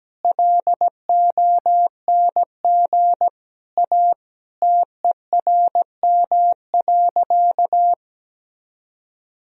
Morse code, heard as a keyed tone, about 17 wpm, 700 Hz. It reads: LONG A TERM.
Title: Sentences from Top 1000 Words - Review Part 4 - 17wpm